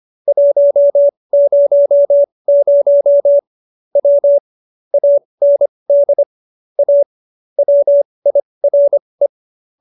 Morse code, heard as a keyed tone, about 25 wpm, spaced as for 20 wpm, 570 Hz. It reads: 100 W AND A WIRE